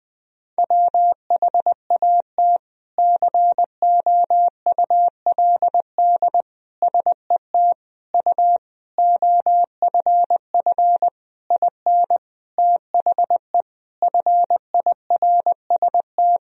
Morse code, heard as a keyed tone, about 20 wpm, 700 Hz. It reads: WHAT COULD SET U OFF IN THE FIRST
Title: Sentences from Top 200 Words and Abbreviations 20wpm